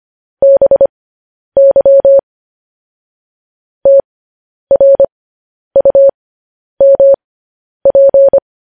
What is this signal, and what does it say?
Morse code, heard as a keyed tone, about 25 wpm, spaced as for 10 wpm, 550 Hz. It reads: BY TRUMP
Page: news headlines in morse code at 10 wpm